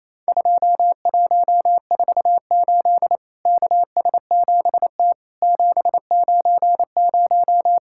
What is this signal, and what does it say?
Morse code, heard as a keyed tone, about 28 wpm, 700 Hz. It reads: 2148 KS7T 790
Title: Worked All Europe DX Contest - QTC Exchange - 28wpm